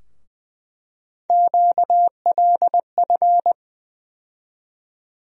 Morse code, heard as a keyed tone, about 20 wpm, 700 Hz. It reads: QLF
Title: USA Call Sign Postfixes - 3 Letters 20wpm